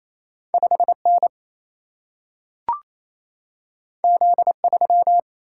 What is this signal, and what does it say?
Morse code, heard as a keyed tone, about 28 wpm, 700 Hz. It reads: 5N Z3
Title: Single-Letter Single-Number 28wpm